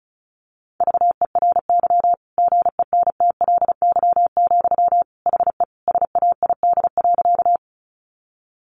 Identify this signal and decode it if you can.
Morse code, heard as a keyed tone, about 34 wpm, 700 Hz. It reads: VERY GENTLY, HE SAID.
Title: Sentences from Top 1500 Words 35wpm